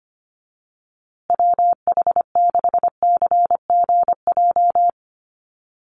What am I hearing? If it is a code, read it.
Morse code, heard as a keyed tone, about 25 wpm, 700 Hz. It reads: WH6CGJ
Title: USA Call Signs Repeated 3x 25wpm